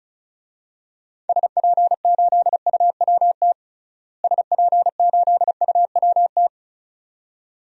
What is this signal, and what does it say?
Morse code, heard as a keyed tone, about 36 wpm, 700 Hz. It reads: SP8UWT SP8UWT
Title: DXCC-100 Call Signs - 6 Characters - Repeated 2x - 35wpm